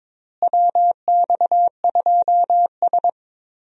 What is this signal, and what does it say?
Morse code, heard as a keyed tone, about 21 wpm, 700 Hz. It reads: WX2S
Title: USA Call Signs - 4 Characters - Repeated 3x - 22wpm